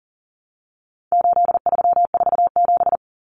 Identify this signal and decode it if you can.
Morse code, heard as a keyed tone, about 40 wpm, 700 Hz. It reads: 8347
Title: Numbers - 4-Digit 40wpm